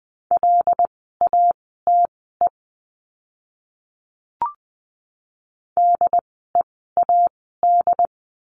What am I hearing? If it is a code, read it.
Morse code, heard as a keyed tone, about 20 wpm, spaced as for 15 wpm, 700 Hz. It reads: LATE DEAD
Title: 075 Lesson 13f - Letter L - Words 20wpm